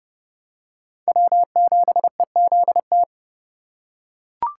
Morse code, heard as a keyed tone, about 29 wpm, 700 Hz. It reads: W7EZT